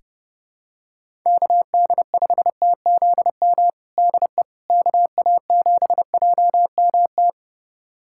Morse code, heard as a keyed tone, about 30 wpm, 700 Hz. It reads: KD5TZM DE KA7JMT
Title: USA Callsign DE Callsign 30wpm